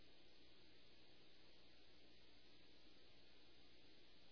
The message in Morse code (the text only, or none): none